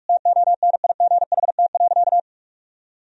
Morse code, heard as a keyed tone, about 42 wpm, 700 Hz. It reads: TONIGHT.
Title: Sentences from Top 1000 Words - Review Part 4 - 45wpm